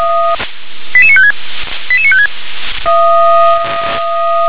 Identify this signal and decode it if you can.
Five-tone selective call (ZVEI), heard as quick runs of five tones, tones of about 70 ms. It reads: 88956 88956